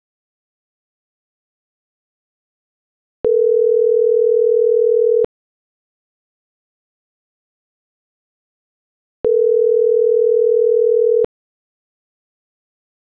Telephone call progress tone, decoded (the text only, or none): ringback tone